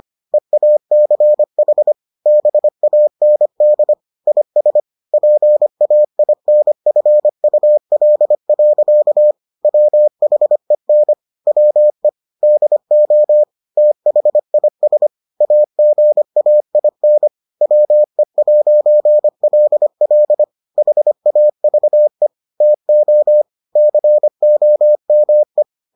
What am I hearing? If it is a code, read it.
Morse code, heard as a keyed tone, about 25 wpm, 600 Hz. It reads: EACH BAND IS PAINFUL. WHEN WE DO THIS AGAIN WE'LL HAVE TO COME